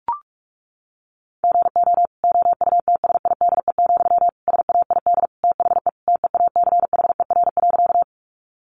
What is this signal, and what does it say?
Morse code, heard as a keyed tone, about 44 wpm, 700 Hz. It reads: GO OUTSIDE, SAID THE TEACHER.
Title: Sentences from Top 900 Words 45wpm